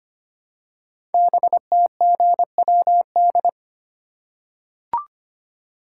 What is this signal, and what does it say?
Morse code, heard as a keyed tone, about 25 wpm, 700 Hz. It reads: BTGWD
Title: Code Groups - 5 Letters - 25wpm